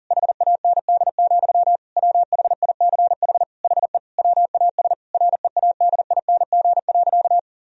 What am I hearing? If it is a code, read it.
Morse code, heard as a keyed tone, about 40 wpm, 700 Hz. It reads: HAND, WHICH HE WAS READING.
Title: Sentences from Top 1000 Words - Review Part 3 - 40wpm